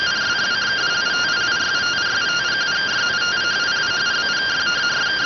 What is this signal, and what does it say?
Two-tone teleprinter signal, about 170 Hz shift.